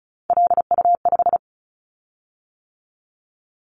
Morse code, heard as a keyed tone, about 36 wpm, 690 Hz. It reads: LU5
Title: Callsign Prefixes for Top 50 DXCC - 35wpm